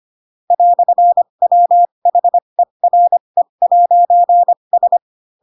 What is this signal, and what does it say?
Morse code, heard as a keyed tone, about 25 wpm, spaced as for 22 wpm, 700 Hz. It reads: "WHERE'S